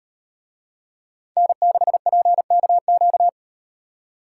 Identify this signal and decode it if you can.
Morse code, heard as a keyed tone, about 38 wpm, 700 Hz. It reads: N6PKQ